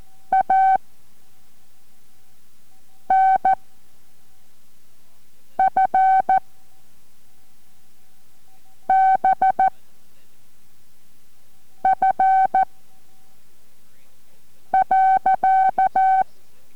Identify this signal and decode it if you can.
Morse code, heard as a keyed tone, about 14 wpm, 760 Hz. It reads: A N F B F .